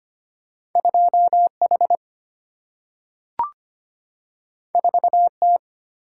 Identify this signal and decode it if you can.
Morse code, heard as a keyed tone, about 25 wpm, 700 Hz. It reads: 2H 4T
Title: Single-Letter Single-Number 25wpm